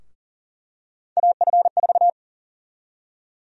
Morse code, heard as a keyed tone, about 39 wpm, 700 Hz. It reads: AF4
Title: USA Call Sign Prefixes - Repeated 2x - 40wpm